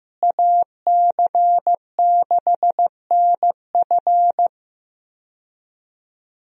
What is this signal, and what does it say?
Morse code, heard as a keyed tone, about 15 wpm, 700 Hz. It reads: AC6NF
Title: USA Call Signs - 5 Characters - Repeated 2x - 15wpm